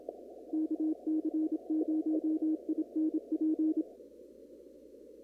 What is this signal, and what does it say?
Morse code, heard as a keyed tone, about 27 wpm, 310 Hz. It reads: KC0INP
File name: CW / kt5j wpx cw-op-k5pi